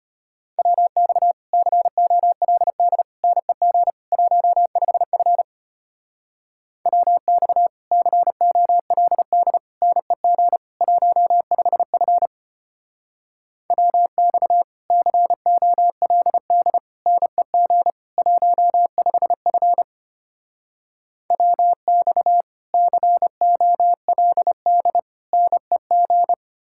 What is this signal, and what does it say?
Morse code, heard as a keyed tone, about 33 wpm, 700 Hz. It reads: WX COLD NEG 15F WX COLD NEG 15F WX COLD NEG 15F WX COLD NEG